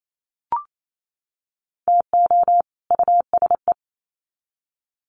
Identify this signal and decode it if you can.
Morse code, heard as a keyed tone, about 28 wpm, 700 Hz. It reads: TO USE
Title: Sets of 2 Words 28wpm